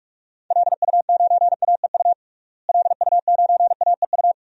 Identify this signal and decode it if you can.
Morse code, heard as a keyed tone, about 46 wpm, 700 Hz. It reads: LU9AEU LU9AEU
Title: DXCC-100 Call Signs - 6 Characters - Repeated 2x - 45wpm